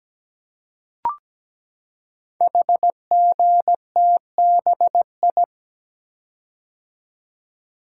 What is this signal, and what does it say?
Morse code, heard as a keyed tone, about 17 wpm, 700 Hz. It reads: HGTBI
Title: Code Groups - 5 Letters - 17wpm